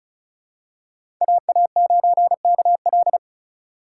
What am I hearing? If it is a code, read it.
Morse code, heard as a keyed tone, about 36 wpm, 700 Hz. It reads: AA9KL
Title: USA Call Signs - 5 Characters - 3x Character Spacing - 35wpm